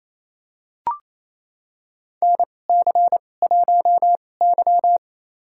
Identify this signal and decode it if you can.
Morse code, heard as a keyed tone, about 28 wpm, spaced as for 20 wpm, 700 Hz. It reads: NC1Y
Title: USA Call Signs - 4 Characters - 2x Character Spacing - 28wpm